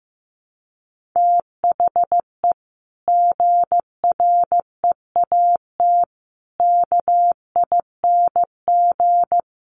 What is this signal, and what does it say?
Morse code, heard as a keyed tone, about 15 wpm, 700 Hz. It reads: THE GREAT KING
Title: Sentences from Top 1500 Words 15wpm